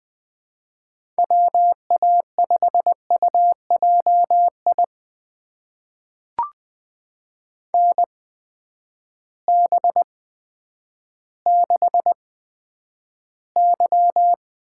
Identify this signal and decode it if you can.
Morse code, heard as a keyed tone, about 20 wpm, spaced as for 10 wpm, 700 Hz. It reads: WA5UJI NB6Y
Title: USA Call Signs - 8x Character Spacing - 20wpm